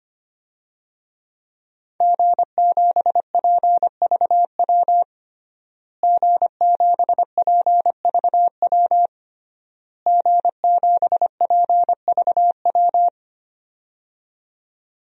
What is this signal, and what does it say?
Morse code, heard as a keyed tone, about 25 wpm, 700 Hz. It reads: G7PVW G7PVW G7PVW
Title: DXCC-100 Call Signs - 5 Characters - Repeated 3x - 25wpm